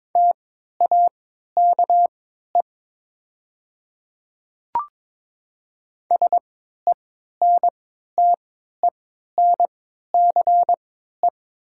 Morse code, heard as a keyed tone, about 22 wpm, spaced as for 12 wpm, 700 Hz. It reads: TAKE SENTENCE
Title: 3x Character Spacing - Top 200 Words - 22wpm